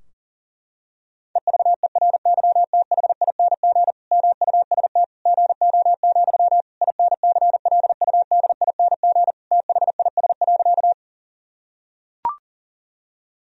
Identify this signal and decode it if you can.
Morse code, heard as a keyed tone, about 37 wpm, 700 Hz. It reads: EVERYTHING MUST GO, INCLUDING THIS.
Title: Sentences from Top 1000 Words - Review Part 4 - 40wpm